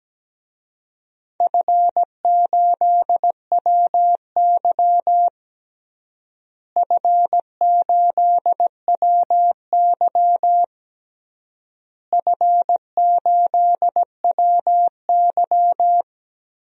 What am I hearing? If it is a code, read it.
Morse code, heard as a keyed tone, about 17 wpm, 700 Hz. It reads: F8WY F8WY F8WY
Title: DXCC-100 Call Signs - 4 Characters - Repeated 3x - 17wpm